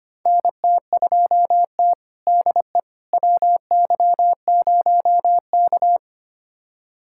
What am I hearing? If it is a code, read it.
Morse code, heard as a keyed tone, about 25 wpm, 700 Hz. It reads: NT2T DE WY0K
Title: USA Callsign DE Callsign - 4 Characters - 25wpm